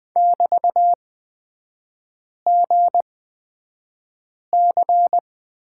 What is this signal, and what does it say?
Morse code, heard as a keyed tone, about 20 wpm, spaced as for 5 wpm, 700 Hz. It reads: <BT>GC
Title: 224 Lesson 36e - BT - Three Characters 20wpm